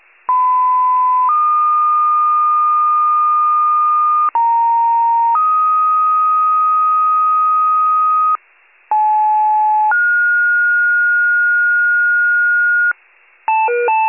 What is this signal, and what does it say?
Two-tone page, heard as a one-second tone, then a three-second tone, about 1010 then 1220 Hz, about 930 then 1250 Hz, about 860 then 1430 Hz. It is heard from 0.3 to 12.9 s.